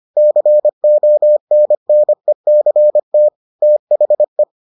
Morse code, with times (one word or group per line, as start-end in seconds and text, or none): 0.2-3.3 CONNECT
3.6-4.4 THE